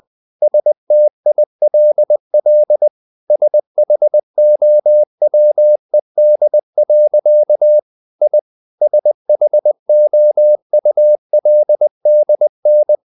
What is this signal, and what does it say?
Morse code, heard as a keyed tone, about 20 wpm, 600 Hz. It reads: STILL SHOWED. I SHOULDN
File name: the old man and the sea part 2 20wpm